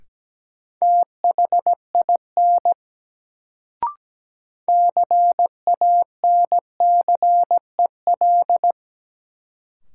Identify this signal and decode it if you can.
Morse code, heard as a keyed tone, about 17 wpm, 700 Hz. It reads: THIN CANCEL
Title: Sentences from Top 1200 Words - New Word Drill - 17wpm